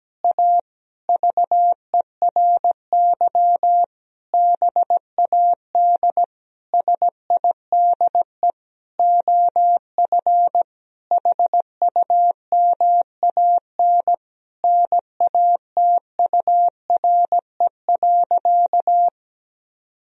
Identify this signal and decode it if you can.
Morse code, heard as a keyed tone, about 17 wpm, 700 Hz. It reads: A VERY BAD SIDE OF HUMAN NATURE.
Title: Sentences from Top 1000 Words - Review Part 3 - 17wpm